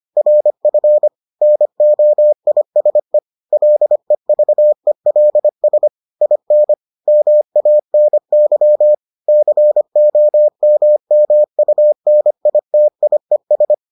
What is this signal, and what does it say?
Morse code, heard as a keyed tone, about 25 wpm, 600 Hz. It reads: RF NOISE LEVELS IN MANY COMMUNITIES